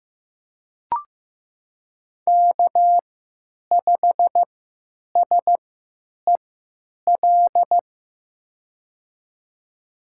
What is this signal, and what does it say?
Morse code, heard as a keyed tone, about 15 wpm, spaced as for 8 wpm, 700 Hz. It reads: K5SEL
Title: USA Call Signs - 3x Character Spacing - 15wpm